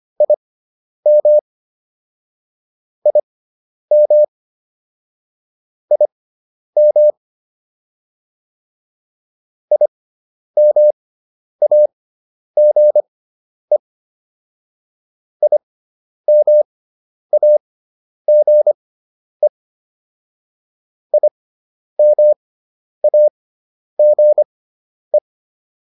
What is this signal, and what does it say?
Morse code, heard as a keyed tone, about 25 wpm, spaced as for 10 wpm, 600 Hz. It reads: IM IM IM IMAGE IMAGE IMAGE